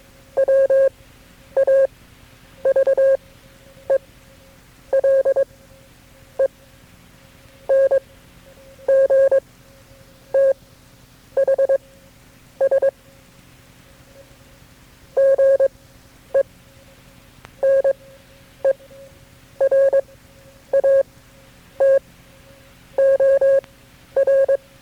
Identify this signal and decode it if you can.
Morse code, heard as a keyed tone, about 22 wpm, spaced as for 8 wpm, 540 Hz. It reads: WAVELENGTHS GENERATOR
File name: FreeFiles / CW-practice / KB6MT-tape3-side1